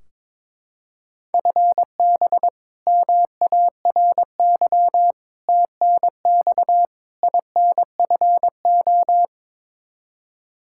Morse code, heard as a keyed tone, about 22 wpm, 700 Hz. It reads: FB MARY TNX INFO
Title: Practice QSO - Exchange 3 - More Info - 22wpm